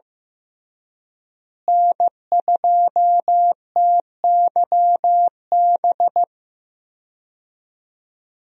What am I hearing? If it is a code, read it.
Morse code, heard as a keyed tone, about 16 wpm, 700 Hz. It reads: N2TYB
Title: USA Call Signs - 5 Characters - Repeated 3x - 15wpm